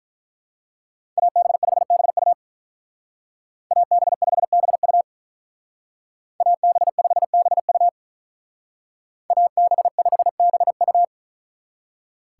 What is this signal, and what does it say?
Morse code, heard as a keyed tone, about 44 wpm, 700 Hz. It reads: AB5BU AB5BU AB5BU AB5BU